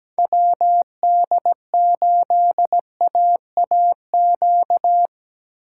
Morse code, heard as a keyed tone, about 17 wpm, 700 Hz. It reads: WD8AAQ
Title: USA Call Signs - 6 Characters - 3x Character Spacing - 17wpm